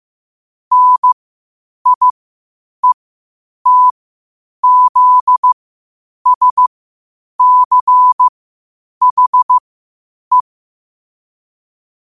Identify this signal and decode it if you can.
Morse code, heard as a keyed tone, about 15 wpm, spaced as for 8 wpm, 1000 Hz. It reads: NIETZSCHE
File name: qotdfiles / 8wpm